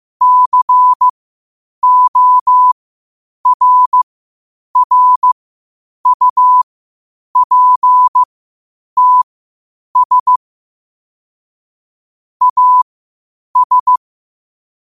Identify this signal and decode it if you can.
Morse code, heard as a keyed tone, about 15 wpm, spaced as for 8 wpm, 1000 Hz. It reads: CORRUPTS AS
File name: qotdfiles / 8wpm